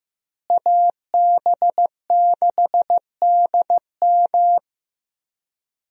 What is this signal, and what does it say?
Morse code, heard as a keyed tone, about 15 wpm, 700 Hz. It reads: AB6DM
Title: USA Call Signs - 5 Characters - Repeated 2x - 15wpm